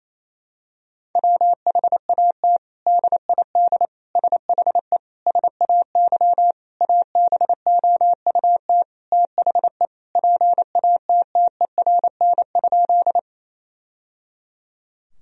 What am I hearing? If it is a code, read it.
Morse code, heard as a keyed tone, about 28 wpm, 700 Hz. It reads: WHAT DID SHE SAY ABOUT THE PATTERN?
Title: Sentences from Top 1000 Words - Review Part 2 - 28wpm